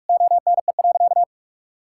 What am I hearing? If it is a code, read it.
Morse code, heard as a keyed tone, about 46 wpm, 700 Hz. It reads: ONE.
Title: Sentences from Top 1000 Words - Review Part 5 - 45wpm